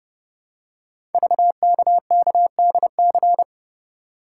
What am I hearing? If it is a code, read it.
Morse code, heard as a keyed tone, about 30 wpm, 700 Hz. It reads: VKKDC
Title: Code Groups - 5 Letters - 30wpm